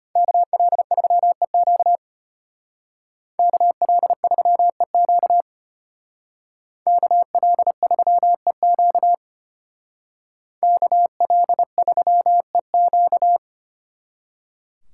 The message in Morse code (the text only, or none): KL3EQ KL3EQ KL3EQ KL3EQ